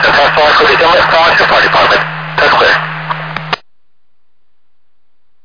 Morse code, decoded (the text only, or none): none